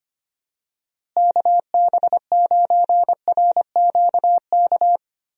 Morse code, heard as a keyed tone, about 25 wpm, 700 Hz. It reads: KB9RQK
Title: USA Call Signs - 6 Characters - Repeated 2x - 25wpm